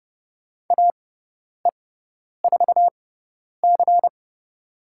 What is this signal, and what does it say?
Morse code, heard as a keyed tone, about 30 wpm, spaced as for 10 wpm, 690 Hz. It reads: AE4C
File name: Lesson 18g - Letter m - Call Signs 30wpm